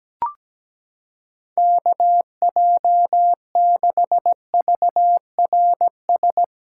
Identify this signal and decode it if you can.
Morse code, heard as a keyed tone, about 17 wpm, 700 Hz. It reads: KJ6VRS
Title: USA Callsign DE Callsign - 6 Characters - 17wpm